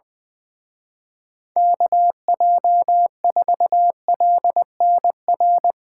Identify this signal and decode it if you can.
Morse code, heard as a keyed tone, about 20 wpm, 700 Hz. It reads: KJ4LNR
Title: Speed Racing - USA 2x3 Call Signs 20wpm